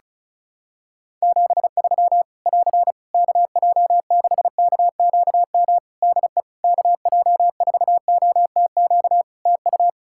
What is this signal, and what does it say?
Morse code, heard as a keyed tone, about 34 wpm, 700 Hz. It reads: 73 <AR> KJ6KQM DE KJ4OTQ TU
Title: Practice QSO - Exchange 4 - Ending - 35wpm